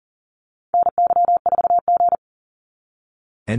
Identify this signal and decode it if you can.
Morse code, heard as a keyed tone, about 40 wpm, 700 Hz. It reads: NY4G
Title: NA SOTA Activator Callsigns 40wpm